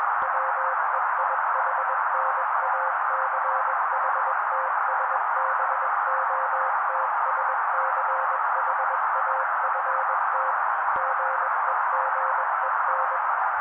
Morse code, heard as a keyed tone, about 21 wpm, 530 Hz. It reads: WEIHNACHTSBOTSCHAFT GEGEN